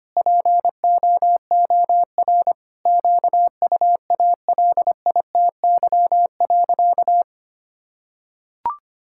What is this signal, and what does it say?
Morse code, heard as a keyed tone, about 25 wpm, 700 Hz. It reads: POOR QUALITY.
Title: Sentences from Top 1000 Words - Review Part 5 - 25wpm